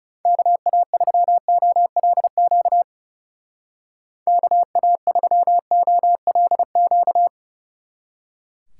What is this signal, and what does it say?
Morse code, heard as a keyed tone, about 35 wpm, 700 Hz. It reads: KA3OLQ KA3OLQ
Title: Speed Racing - USA 2x3 Call Signs 30wpm